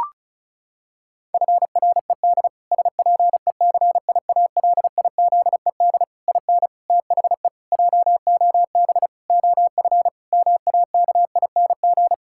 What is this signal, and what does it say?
Morse code, heard as a keyed tone, about 34 wpm, 700 Hz. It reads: FRED SPECIALIZED IN THE JOB OF MAKING
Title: Pangram Challenge v2 - 35wpm